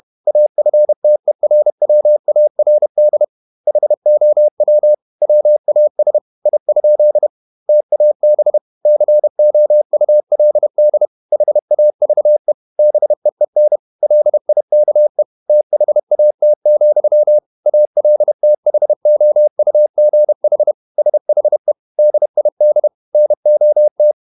Morse code, heard as a keyed tone, about 31 wpm, 600 Hz. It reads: AFTERWARD HOW WAS I? TAB COULD HAVE BEEN LIKE THAT, ALTHOUGH SHE DID NOT
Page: Never 31wpm | CW med Gnister